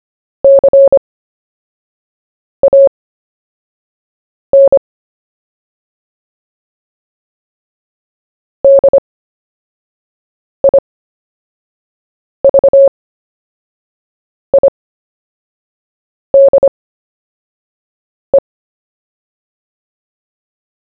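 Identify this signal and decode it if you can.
Morse code, heard as a keyed tone, about 25 wpm, spaced as for 5 wpm, 550 Hz. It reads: CAN DIVIDE
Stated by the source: CW story-telling practice, Short Story 1 sent at 5 wpm, by VK4JE, Queensland, Australia.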